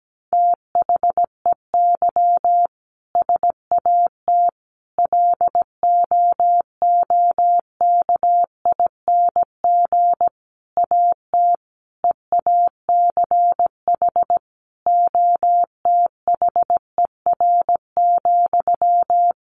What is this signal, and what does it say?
Morse code, heard as a keyed tone, about 17 wpm, 700 Hz. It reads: THEY SAT LOOKING AT EACH OTHER,
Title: A Sound Of Thunder 17wpm